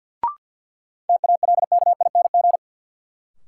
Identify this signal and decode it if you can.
Morse code, heard as a keyed tone, about 52 wpm, 700 Hz. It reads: TALKING
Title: Sets of 2 Words - New Word Drill - 50wpm